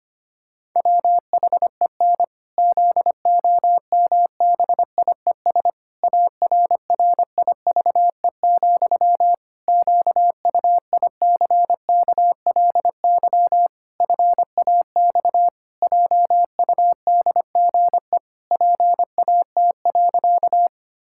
Morse code, heard as a keyed tone, about 25 wpm, 700 Hz. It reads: WHEN ZOMBIES ARRIVE, QUICKLY FAX JUDGE PAT.
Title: Pangram Challenge v2 - 25wpm